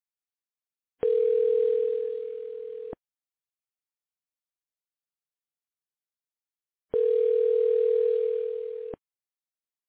Ringback tone (North America), 440 + 480 Hz.